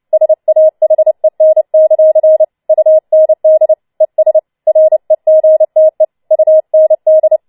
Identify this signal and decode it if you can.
Morse code, heard as a keyed tone, about 29 wpm, 630 Hz. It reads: SAHEN; UND ES REGTE UND
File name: Schneewittchen Teil3 30wpm